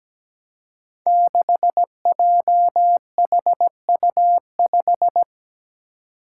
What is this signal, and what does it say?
Morse code, heard as a keyed tone, about 17 wpm, 700 Hz. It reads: 6JHU5
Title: Five Challenging Letter-Number Combinations 17wpm